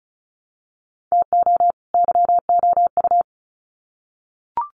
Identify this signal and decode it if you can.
Morse code, heard as a keyed tone, about 34 wpm, 700 Hz. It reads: TO YOU
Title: Sets of 2 Words 35wpm